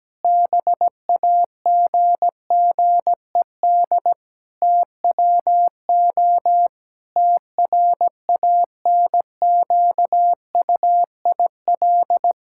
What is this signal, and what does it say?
Morse code, heard as a keyed tone, about 17 wpm, 700 Hz. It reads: BAGGED TWO TRANQUIL
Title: Pangram Challenge v2 - 17wpm